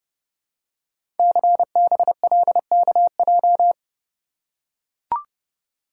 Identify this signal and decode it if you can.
Morse code, heard as a keyed tone, about 30 wpm, 700 Hz. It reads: CBLKJ
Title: Code Groups - 5 Letters - 30wpm